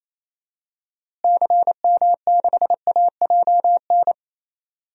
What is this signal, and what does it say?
Morse code, heard as a keyed tone, about 28 wpm, 700 Hz. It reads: CM6AJN
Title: DXCC-100 Call Signs - 6 Characters - Repeated 3x - 28wpm